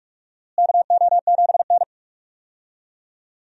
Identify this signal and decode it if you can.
Morse code, heard as a keyed tone, about 42 wpm, 700 Hz. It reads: KO7N
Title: USA Call Signs - 4 Characters - Repeated 3x - 45wpm